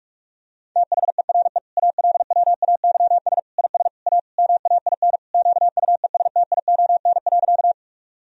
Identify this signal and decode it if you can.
Morse code, heard as a keyed tone, about 41 wpm, 700 Hz. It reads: THERE ALWAYS IS A MAIN QUESTION.
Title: Sentences from Top 1000 Words - Review Part 2 - 45wpm